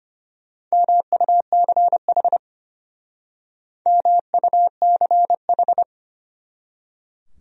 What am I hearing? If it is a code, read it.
Morse code, heard as a keyed tone, about 32 wpm, 700 Hz. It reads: MUCH MUCH